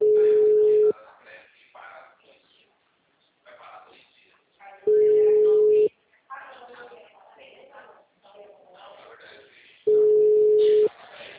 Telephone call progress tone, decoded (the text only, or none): ringback tone